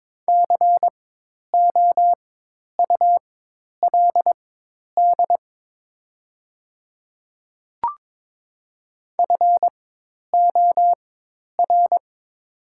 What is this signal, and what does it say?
Morse code, heard as a keyed tone, about 22 wpm, spaced as for 10 wpm, 700 Hz. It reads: COULD FOR
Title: 4x Character Spacing - Top 200 Words - 22wpm